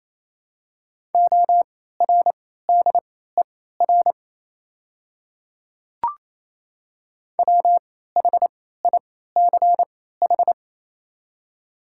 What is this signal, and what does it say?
Morse code, heard as a keyed tone, about 28 wpm, spaced as for 16 wpm, 690 Hz. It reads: ORDER WHICH
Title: 3x Character Spacing - Top 500 Words - 28wpm